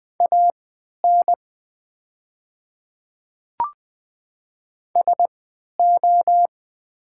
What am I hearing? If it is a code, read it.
Morse code, heard as a keyed tone, about 20 wpm, spaced as for 11 wpm, 690 Hz. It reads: AN SO
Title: 3x Character Spacing - Top 200 Words - 20wpm